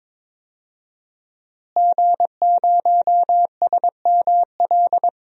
Morse code, heard as a keyed tone, about 22 wpm, 700 Hz. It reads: G0SML
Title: DXCC-100 Call Signs - 5 Characters - Repeated 2x - 22wpm